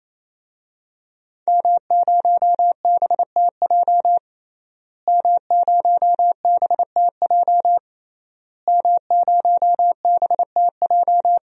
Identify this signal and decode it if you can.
Morse code, heard as a keyed tone, about 28 wpm, 700 Hz. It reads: M0BTJ M0BTJ M0BTJ
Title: DXCC-100 Call Signs - 5 Characters - Repeated 3x - 28wpm